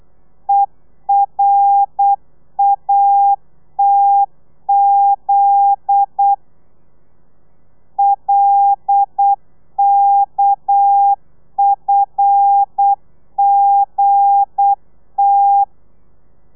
Morse code, heard as a keyed tone, about 8 wpm, 800 Hz. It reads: ERATZ LKFGT
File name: Lez11